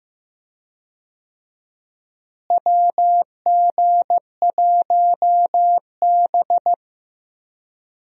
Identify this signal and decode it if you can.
Morse code, heard as a keyed tone, about 15 wpm, 700 Hz. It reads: WG1B
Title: USA Call Signs - 4 Characters - Repeated 2x - 15wpm